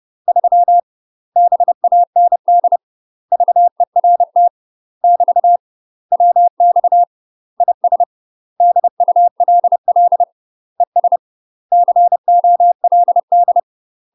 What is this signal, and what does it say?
Morse code, heard as a keyed tone, about 30 wpm, 700 Hz. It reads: 3 BAND VERT <BT> WX IS DULL ES COLD
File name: UK MASTER QSOs TRAINING / 30WPM / UK MASTER QSOs 0002